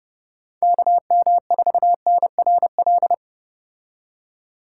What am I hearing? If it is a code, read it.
Morse code, heard as a keyed tone, about 34 wpm, 700 Hz. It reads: KM4NRL